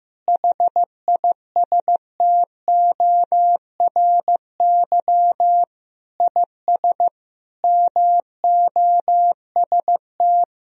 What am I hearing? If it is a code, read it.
Morse code, heard as a keyed tone, about 15 wpm, 700 Hz. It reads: HISTORY IS MOST
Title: Sentences from Top 1000 Words - Review Part 4 - 15wpm